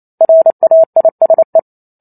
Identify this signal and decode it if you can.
Morse code, heard as a keyed tone, about 28 wpm, 650 Hz. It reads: RAISE
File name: Chapter 2 - 28 WPM